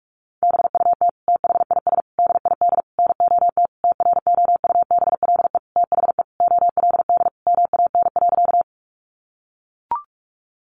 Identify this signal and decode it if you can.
Morse code, heard as a keyed tone, about 44 wpm, 700 Hz. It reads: BUT THIS DID NOT TROUBLE THE OLD MAN.
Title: Sentences from Top 1200 Words 45wpm